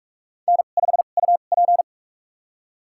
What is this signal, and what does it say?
Morse code, heard as a keyed tone, about 44 wpm, spaced as for 32 wpm, 700 Hz. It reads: N5UP
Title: USA Call Signs - 4 Characters - 2x Character Spacing - 45wpm